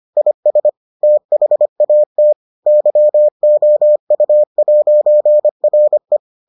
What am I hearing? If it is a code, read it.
Morse code, heard as a keyed tone, about 25 wpm, 600 Hz. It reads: IS THAT YOU'RE